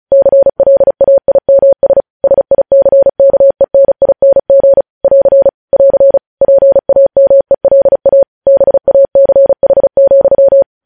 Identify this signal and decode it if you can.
Morse code, heard as a keyed tone, about 36 wpm, 550 Hz. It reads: CLAIMS SICKENING <AR> <AR> PAMELA BACH,
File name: podcast / News headlines-1x-35x35